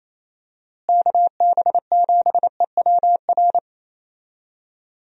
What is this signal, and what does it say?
Morse code, heard as a keyed tone, about 28 wpm, 700 Hz. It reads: KB7EWR
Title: USA Call Signs - 6 Characters - Repeated 2x - 28wpm